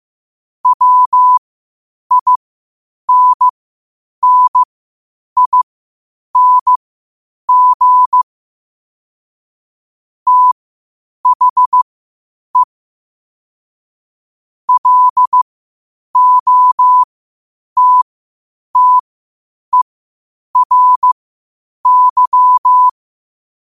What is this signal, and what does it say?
Morse code, heard as a keyed tone, about 15 wpm, spaced as for 8 wpm, 1000 Hz. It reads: WINNING THE LOTTERY
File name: qotdfiles / 8wpm